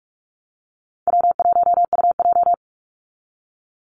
Morse code, heard as a keyed tone, about 44 wpm, 700 Hz. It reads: W1UJ
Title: Call Signs Repeated 3x - 45wpm